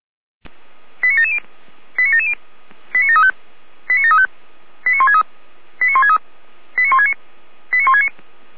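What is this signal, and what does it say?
Five-tone selective call (ZVEI), heard as quick runs of five tones, tones of about 80 ms. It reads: 89770 89770 89735 89735 89163 89163 89169 89169
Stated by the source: Received with an SDR as ZVEI five-tone selective call, code 89725.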